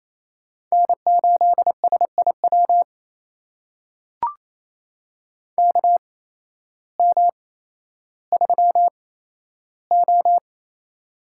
Morse code, heard as a keyed tone, about 28 wpm, spaced as for 13 wpm, 700 Hz. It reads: N8SIW KM3O
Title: USA Call Signs - 8x Character Spacing - 28wpm